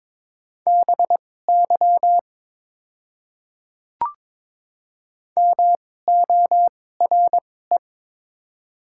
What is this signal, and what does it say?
Morse code, heard as a keyed tone, about 22 wpm, spaced as for 16 wpm, 700 Hz. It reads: BY MORE
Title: 2x Character Spacing - Top 100 Words - 22wpm